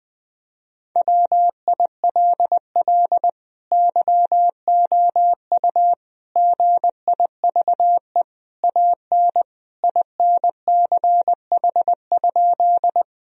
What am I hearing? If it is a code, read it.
Morse code, heard as a keyed tone, about 20 wpm, 700 Hz. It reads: WILL YOU GIVE AN INCH?
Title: Sentences from Top 1000 Words - Review Part 3 - 20wpm